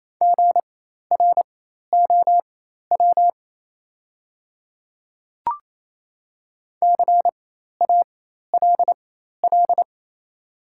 Morse code, heard as a keyed tone, about 27 wpm, spaced as for 13 wpm, 690 Hz. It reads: GROW CALL
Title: 4x Character Spacing - Top 300 Words - 28wpm